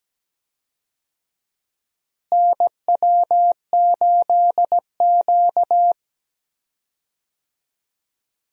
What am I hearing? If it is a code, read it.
Morse code, heard as a keyed tone, about 17 wpm, 700 Hz. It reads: NW8Q
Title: USA Call Signs - 4 Characters - Repeated 3x - 17wpm